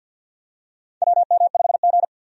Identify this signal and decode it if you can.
Morse code, heard as a keyed tone, about 52 wpm, 700 Hz. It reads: WM5G
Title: USA Call Signs - 4 Characters - 50wpm